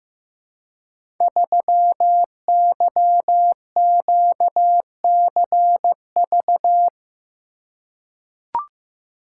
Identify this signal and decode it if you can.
Morse code, heard as a keyed tone, about 15 wpm, 700 Hz. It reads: 3YQCV
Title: Code Groups - 5 Letter:Numbers - 15wpm